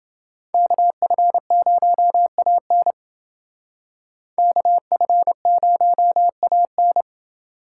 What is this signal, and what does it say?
Morse code, heard as a keyed tone, about 28 wpm, 700 Hz. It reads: KF0AN KF0AN